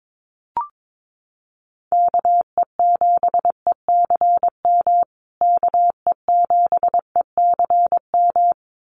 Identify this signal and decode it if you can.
Morse code, heard as a keyed tone, about 22 wpm, 700 Hz. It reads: KE7ECM KE7ECM
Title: USA Call Signs Repeated 2x 22wpm